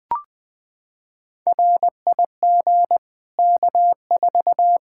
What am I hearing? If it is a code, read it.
Morse code, heard as a keyed tone, about 20 wpm, 700 Hz. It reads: RIG K4
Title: Practice QSO - Exchange 2 - Info - 20wpm